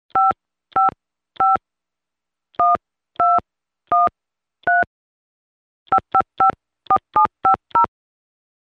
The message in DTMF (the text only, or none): 555 1213 5554750